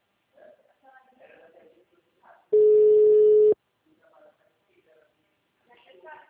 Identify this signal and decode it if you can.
Ringback tone (Europe, 425 Hz), with speech.
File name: out- -2018- - - .40